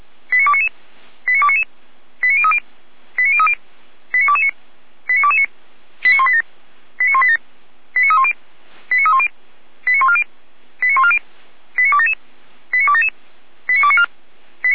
Five-tone selective call (ZVEI), heard as quick runs of five tones, tones of about 70 ms. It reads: 89299 89299 89030 89030 89229 89229 89187 89187 89310 89310 89140 89140 89277 89277 89294